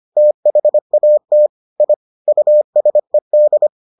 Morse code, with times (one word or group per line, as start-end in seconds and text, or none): 0.2-1.5 THAT
1.8-1.9 I
2.3-3.7 USED